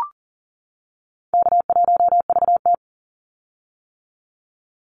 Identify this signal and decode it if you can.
Morse code, heard as a keyed tone, about 38 wpm, 700 Hz. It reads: K1VT
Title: Call Signs - 40wpm